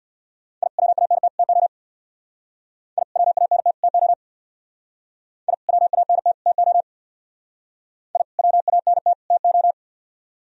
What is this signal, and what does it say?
Morse code, heard as a keyed tone, about 60 wpm, 700 Hz. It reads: I WANT TO I WANT TO I WANT TO I WANT TO
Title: Speed-Racing - Sets of 3 Words 50wpm